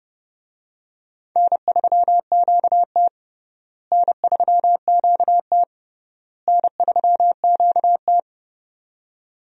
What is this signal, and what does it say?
Morse code, heard as a keyed tone, about 29 wpm, 700 Hz. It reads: N3QT N3QT N3QT
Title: USA Call Signs - 4 Characters - Repeated 3x - 30wpm